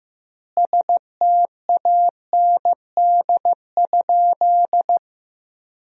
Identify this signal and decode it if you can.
Morse code, heard as a keyed tone, about 15 wpm, 700 Hz. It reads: STAND?
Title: 3x Word-Spacing - Sentences from Top 300 Words - 15wpm